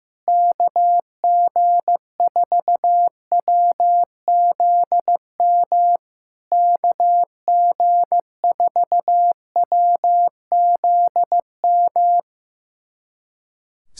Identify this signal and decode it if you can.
Morse code, heard as a keyed tone, about 15 wpm, 700 Hz. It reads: KG4WZM KG4WZM
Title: SOTA QSO Practice 15wpm